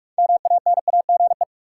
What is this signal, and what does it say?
Morse code, heard as a keyed tone, about 46 wpm, 700 Hz. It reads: MANAGE
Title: Sentences from Top 1400 Words - New Word Drill Alt - 45wpm